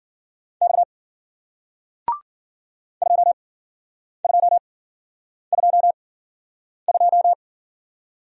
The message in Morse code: X 2222

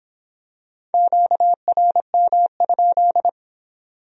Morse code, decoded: QRM?

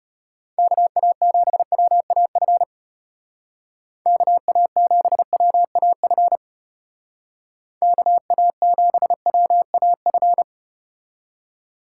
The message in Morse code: KA7WAF KA7WAF KA7WAF

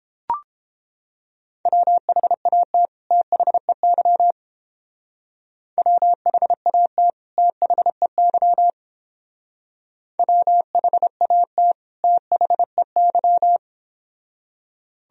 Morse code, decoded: WHAT THEY WHAT THEY WHAT THEY

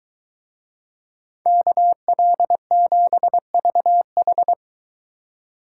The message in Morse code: KL7VH

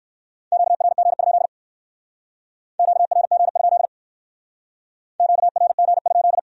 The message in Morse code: QRG? QRG? QRG?

